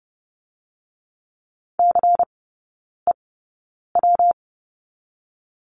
Morse code, heard as a keyed tone, about 30 wpm, 700 Hz. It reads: C E W
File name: Lesson 19e - Letter w - Three Characters 30wpm